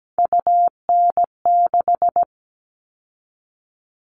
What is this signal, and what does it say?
Morse code, heard as a keyed tone, about 17 wpm, 700 Hz. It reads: UN6
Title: All Cut Numbers - 3-Digit 17wpm